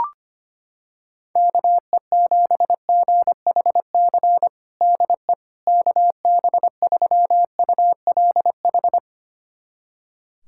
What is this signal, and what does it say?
Morse code, heard as a keyed tone, about 25 wpm, 700 Hz. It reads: KE7GHC DE KB3ULH
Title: USA Callsign DE Callsign - 6 Characters - 25wpm